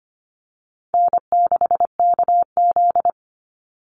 Morse code, heard as a keyed tone, about 25 wpm, 700 Hz. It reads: N6KZ
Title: NA SOTA Chaser Callsigns 25wpm